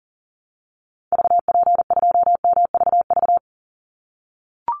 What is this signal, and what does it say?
Morse code, heard as a keyed tone, about 40 wpm, 700 Hz. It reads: VP2MVV
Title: Call Signs - 40wpm